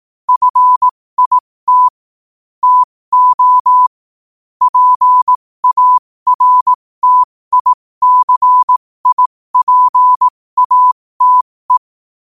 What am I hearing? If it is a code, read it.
Morse code, heard as a keyed tone, about 18 wpm, spaced as for 15 wpm, 1000 Hz. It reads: FIT TO PARTICIPATE
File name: qotdfiles / 15wpm